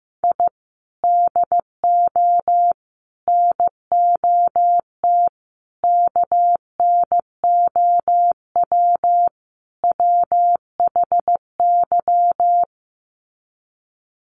Morse code, heard as a keyed tone, about 15 wpm, 700 Hz. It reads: I DO NOT KNOW WHY
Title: Sets of 5 Words 15wpm